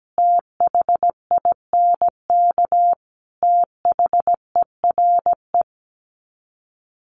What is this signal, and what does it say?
Morse code, heard as a keyed tone, about 17 wpm, 700 Hz. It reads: THINK THERE
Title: Sets of 4 Words 17wpm